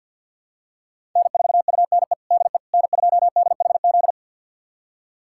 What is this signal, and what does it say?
Morse code, heard as a keyed tone, about 50 wpm, 690 Hz. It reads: N4UNE DE N2DHZ
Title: USA Callsign DE Callsign - 5 Characters - 50wpm